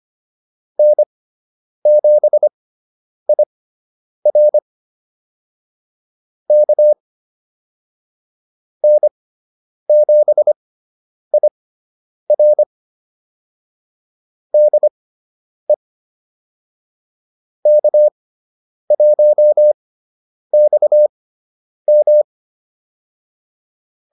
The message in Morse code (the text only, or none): N7IR K N7IR DE K1XM